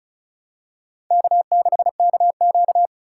Morse code, heard as a keyed tone, about 36 wpm, 700 Hz. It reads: K6KQ
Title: USA Call Signs - 4 Characters - 2x Character Spacing - 35wpm